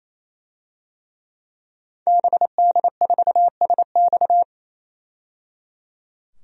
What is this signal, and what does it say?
Morse code, heard as a keyed tone, about 27 wpm, 700 Hz. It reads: BD4SX
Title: DXCC-100 Call Signs - 5 Characters - 28wpm